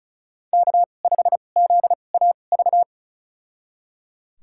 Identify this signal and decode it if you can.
Morse code, heard as a keyed tone, about 36 wpm, spaced as for 25 wpm, 700 Hz. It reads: K5ZAV
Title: USA Call Signs - 5 Characters - 2x Character Spacing - 35wpm